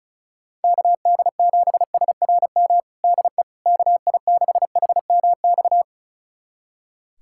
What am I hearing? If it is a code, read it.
Morse code, heard as a keyed tone, about 36 wpm, 700 Hz. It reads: KD7SRM DE KI6HMX
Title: USA Callsign DE Callsign 35wpm